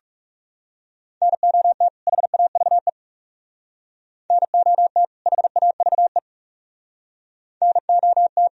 Morse code, heard as a keyed tone, about 36 wpm, 700 Hz. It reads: NOT HAVE NOT HAVE NOT
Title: Speed-Racing - Sets of 2 Words 30wpm